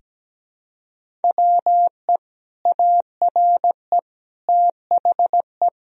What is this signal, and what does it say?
Morse code, heard as a keyed tone, about 17 wpm, 700 Hz. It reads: WE ARE THE
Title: Sentences from Top 1000 Words - Review Part 1 - 17wpm